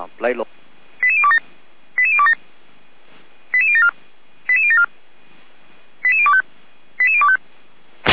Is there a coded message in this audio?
Yes: five-tone selective call (ZVEI) from 1.0 to 7.4 s, heard as quick runs of five tones, tones of about 70 ms, with speech. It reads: 88027 88027 88074 88074 88025 88025